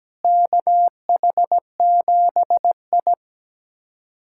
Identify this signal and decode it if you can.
Morse code, heard as a keyed tone, about 17 wpm, 700 Hz. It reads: KH7I
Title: USA Callsign DE Callsign - 4 Characters - 17wpm